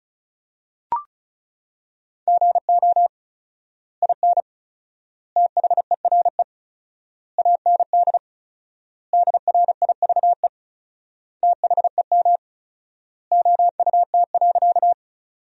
Morse code, heard as a keyed tone, about 36 wpm, 700 Hz. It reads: GO IN THERE AND DRIVE THEM OUT.
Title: 4x Word-Spacing - Sentences from Top 500 Words - 35wpm